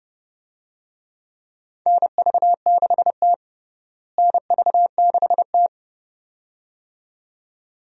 Morse code, heard as a keyed tone, about 32 wpm, 700 Hz. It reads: NV6T NV6T
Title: USA Call Signs - 4 Characters - Repeated 2x - 30wpm